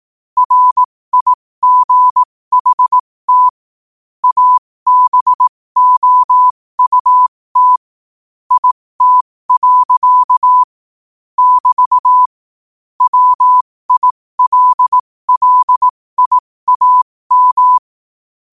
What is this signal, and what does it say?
Morse code, heard as a keyed tone, about 18 wpm, spaced as for 15 wpm, 1000 Hz. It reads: RIGHT ABOUT IT. <BT> WILLIAM
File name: qotdfiles / 15wpm